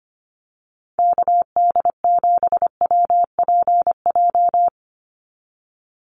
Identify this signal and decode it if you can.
Morse code, heard as a keyed tone, about 25 wpm, 700 Hz. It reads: KD7WPJ
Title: NA SOTA Activator Callsigns 25wpm